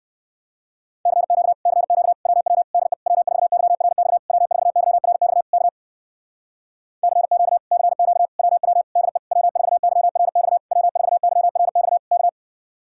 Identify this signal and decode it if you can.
Morse code, heard as a keyed tone, about 65 wpm, 700 Hz. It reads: CQ CQ WW DE W4YRX W4YRX K CQ CQ WW DE W4YRX W4YRX K